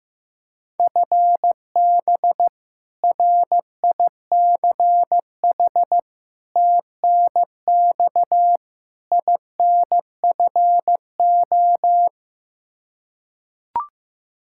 Morse code, heard as a keyed tone, about 15 wpm, 700 Hz. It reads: FB RICH TNX INFO
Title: Practice QSO - Exchange 3 - More Info - 15wpm